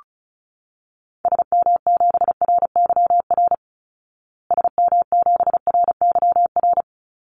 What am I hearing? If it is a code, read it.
Morse code, heard as a keyed tone, about 34 wpm, 700 Hz. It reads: SM7RYR SM7RYR
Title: Call Signs Repeated 2x - 35wpm